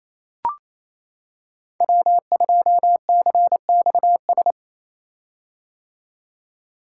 Morse code, heard as a keyed tone, about 29 wpm, 700 Hz. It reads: W2CXS
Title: Code Groups - 5 Letter:Numbers - 28wpm